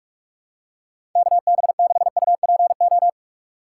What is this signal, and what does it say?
Morse code, heard as a keyed tone, about 46 wpm, 700 Hz. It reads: KB6UPO
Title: USA Call Signs - 6 Characters - 4x Character Spacing - 45wpm